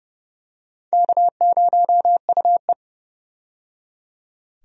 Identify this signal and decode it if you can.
Morse code, heard as a keyed tone, about 30 wpm, 700 Hz. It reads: K0UE
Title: USA Call Signs - 4 Characters - 30wpm